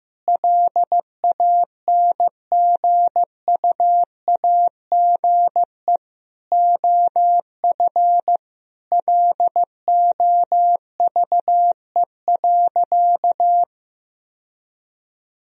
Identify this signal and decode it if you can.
Morse code, heard as a keyed tone, about 15 wpm, 700 Hz. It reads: LANGUAGE OF LOVE.
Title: Sentences from Top 1000 Words - Review Part 3 - 15wpm